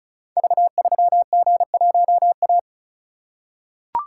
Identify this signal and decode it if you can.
Morse code, heard as a keyed tone, about 36 wpm, 700 Hz. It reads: V3G1A
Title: Code Groups - 5 Letter:Numbers - 35wpm